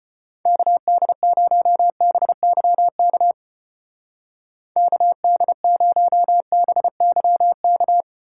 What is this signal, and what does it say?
Morse code, heard as a keyed tone, about 28 wpm, 700 Hz. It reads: KD0BYK KD0BYK
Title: Speed Racing - USA 2x3 Call Signs 25wpm